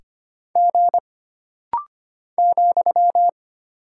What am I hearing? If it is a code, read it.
Morse code, heard as a keyed tone, about 25 wpm, 700 Hz. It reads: G ,
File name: Single Character (Letter:Number:Punctuation:Prosign) - Rapid-Fire - 25wpm